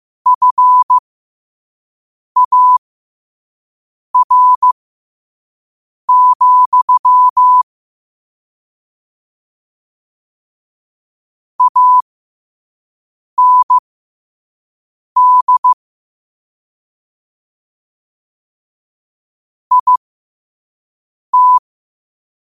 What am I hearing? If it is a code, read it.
Morse code, heard as a keyed tone, about 15 wpm, spaced as for 5 wpm, 1000 Hz. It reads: FAR, AND IT
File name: qotdfiles / 5wpm